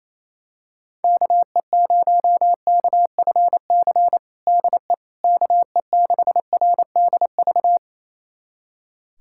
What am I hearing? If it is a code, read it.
Morse code, heard as a keyed tone, about 28 wpm, 700 Hz. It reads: KE0KFC DE KE6RDV
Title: USA Callsign DE Callsign - 6 Characters - 28wpm